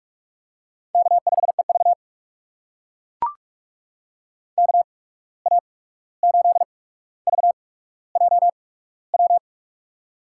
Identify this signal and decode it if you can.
Morse code, heard as a keyed tone, about 44 wpm, spaced as for 17 wpm, 700 Hz. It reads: K5EV KA8VJW
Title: USA Call Signs - 8x Character Spacing - 45wpm